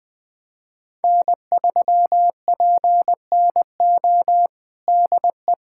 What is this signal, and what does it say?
Morse code, heard as a keyed tone, about 20 wpm, 700 Hz. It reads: N3PNO DE
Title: USA Callsign DE Callsign - 5 Characters - 20wpm